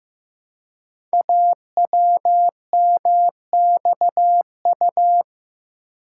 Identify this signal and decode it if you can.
Morse code, heard as a keyed tone, about 16 wpm, 700 Hz. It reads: AWMXU
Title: Code Groups - 5 Letters - 15wpm